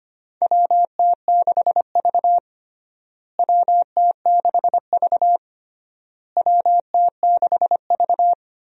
Morse code, heard as a keyed tone, about 25 wpm, 700 Hz. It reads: WT6V WT6V WT6V
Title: USA Call Signs - 4 Characters - Repeated 3x - 25wpm